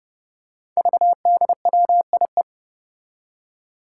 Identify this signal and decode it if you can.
Morse code, heard as a keyed tone, about 30 wpm, 700 Hz. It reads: VDWIE
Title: Five Challenging Letter-Number Combinations 30wpm